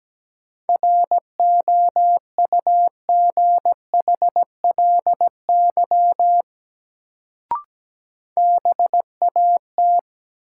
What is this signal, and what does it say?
Morse code, heard as a keyed tone, about 17 wpm, 700 Hz. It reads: ROUGHLY BAT
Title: Sentences from Top 1500 Words - New Word Drill - 17wpm